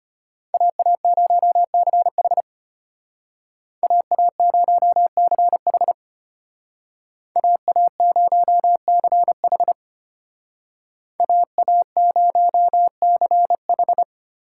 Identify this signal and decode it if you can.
Morse code, heard as a keyed tone, about 31 wpm, 700 Hz. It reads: AA0CH AA0CH AA0CH AA0CH
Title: Speed Racing - USA 2x2 Call Signs 25wpm